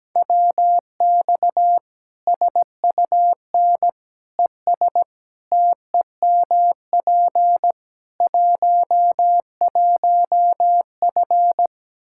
Morse code, heard as a keyed tone, about 17 wpm, 700 Hz. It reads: WX SUN ES TEMP 11F
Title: Practice QSO - Exchange 2 - Info - 17wpm